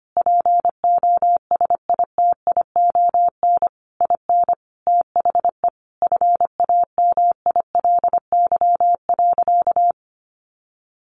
Morse code, heard as a keyed tone, about 25 wpm, 700 Hz. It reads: POSITION IN THE FAMILY.
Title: Sentences from Top 1200 Words 25wpm